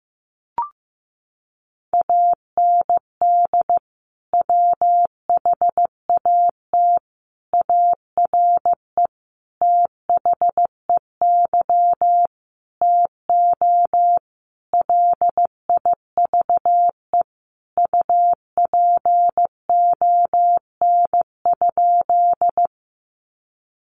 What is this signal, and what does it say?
Morse code, heard as a keyed tone, about 15 wpm, 700 Hz. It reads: AND WHAT ARE THEY TO LIVE UPON?
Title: Sentences from Top 1100 Words 15wpm